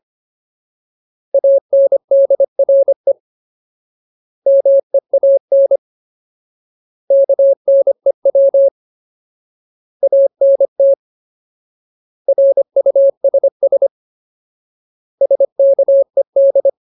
Morse code, heard as a keyed tone, about 25 wpm, 550 Hz. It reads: ANDRE MEAN KNEW ANT RUSS SKED
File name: Ultimate Word List / 25WPM Ultimate